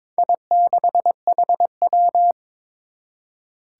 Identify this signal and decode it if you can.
Morse code, heard as a keyed tone, about 22 wpm, 700 Hz. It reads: I6HW
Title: Four Challenging Letter-Number Combinations 22wpm